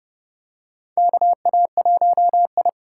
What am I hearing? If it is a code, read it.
Morse code, heard as a keyed tone, about 31 wpm, 700 Hz. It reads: KA1I
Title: USA Call Signs - 4 Characters - 3x Character Spacing - 30wpm